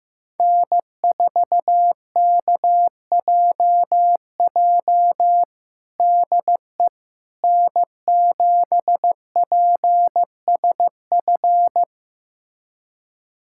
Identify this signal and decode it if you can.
Morse code, heard as a keyed tone, about 15 wpm, 700 Hz. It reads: N4KJJ DE N7PSF
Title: USA Callsign DE Callsign - 5 Characters - 15wpm